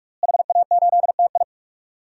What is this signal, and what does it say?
Morse code, heard as a keyed tone, about 46 wpm, 700 Hz. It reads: HA8TI
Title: DXCC-100 Call Signs - 5 Characters - Repeated 3x - 45wpm